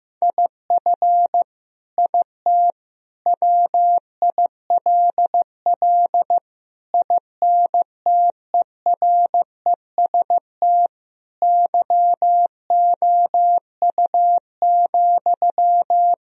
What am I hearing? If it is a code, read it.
Morse code, heard as a keyed tone, about 15 wpm, 700 Hz. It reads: IF IT WILL INTEREST YOU,
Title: Sentences from Top 1000 Words - Review Part 2 - 15wpm